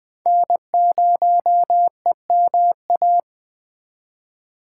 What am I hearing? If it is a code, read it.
Morse code, heard as a keyed tone, about 20 wpm, 700 Hz. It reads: N0EMA
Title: USA Call Signs - 5 Characters - 3x Character Spacing - 20wpm